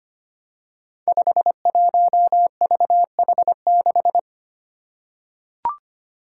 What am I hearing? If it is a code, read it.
Morse code, heard as a keyed tone, about 25 wpm, 700 Hz. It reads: 51VH6
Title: Five Challenging Letter-Number Combinations 25wpm